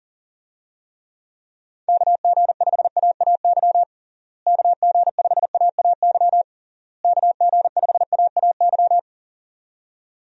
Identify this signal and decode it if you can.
Morse code, heard as a keyed tone, about 37 wpm, 700 Hz. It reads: KG5AAY KG5AAY KG5AAY
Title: USA Call Signs - 6 Characters - Repeated 3x - 40wpm